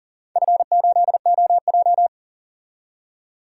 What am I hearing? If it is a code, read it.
Morse code, heard as a keyed tone, about 39 wpm, 700 Hz. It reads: F8OJ
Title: DXCC-100 Call Signs - 4 Characters - 40wpm